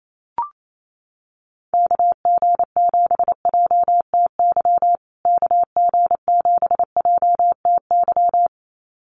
Morse code, heard as a keyed tone, about 28 wpm, 700 Hz. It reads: KG7JTY KG7JTY
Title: USA Call Signs Repeated 2x 28wpm